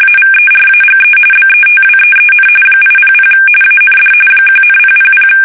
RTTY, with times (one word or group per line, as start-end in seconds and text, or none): 0.0-1.2 AVERAGE
1.3-2.5 BETWEEN
2.8-3.3 800
3.6-4.1 AND
4.5-5.3 1,000